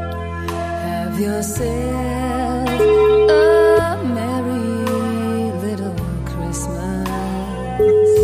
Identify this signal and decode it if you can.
Ringback tone (Europe, 425 Hz), with noise.